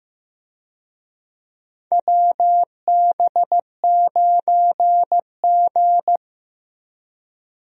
Morse code, heard as a keyed tone, about 15 wpm, 700 Hz. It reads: WB9G
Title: USA Call Signs - 4 Characters - Repeated 2x - 15wpm